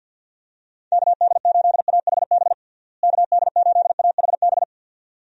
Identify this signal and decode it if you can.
Morse code, heard as a keyed tone, about 51 wpm, 700 Hz. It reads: KD8AHB KD8AHB
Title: USA Call Signs - 6 Characters - Repeated 2x - 50wpm